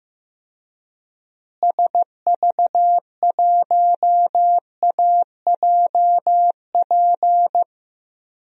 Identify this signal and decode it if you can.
Morse code, heard as a keyed tone, about 15 wpm, 700 Hz. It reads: SV1AJP
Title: DXCC-100 Call Signs - 6 Characters - Repeated 3x - 15wpm